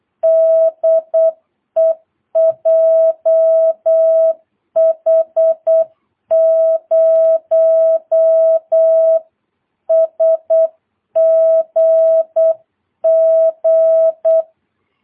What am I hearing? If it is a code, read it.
Morse code, heard as a keyed tone, about 8 wpm, 650 Hz. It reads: DE JH0SGG
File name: jh8cbh / 2022cw